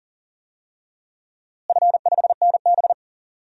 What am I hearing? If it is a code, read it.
Morse code, heard as a keyed tone, about 40 wpm, 700 Hz. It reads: F5NB
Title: DXCC-100 Call Signs - 4 Characters - Repeated 2x - 40wpm